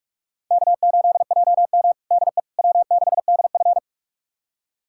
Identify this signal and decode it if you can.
Morse code, heard as a keyed tone, about 41 wpm, 700 Hz. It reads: K8JM DE W6DF
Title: USA Callsign DE Callsign - 4 Characters - 45wpm